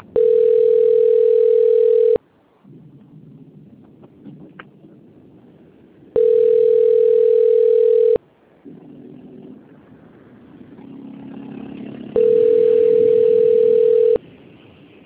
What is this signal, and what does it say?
Ringback tone (North America), 440 + 480 Hz.
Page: Presidente Prudente 4